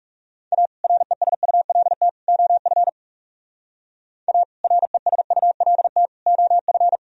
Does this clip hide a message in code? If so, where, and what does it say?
Yes: Morse code from 0.5 to 7.0 s, heard as a keyed tone, about 40 wpm, 700 Hz. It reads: A RESULT OF A RESULT OF